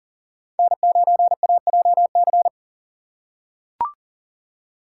Morse code, heard as a keyed tone, about 40 wpm, 700 Hz. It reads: N9AJC
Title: USA Call Signs - 6x Character Spacing - 40wpm